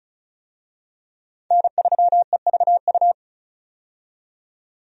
Morse code, heard as a keyed tone, about 36 wpm, 700 Hz. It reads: N3EVU
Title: USA Call Signs - 5 Characters - 35wpm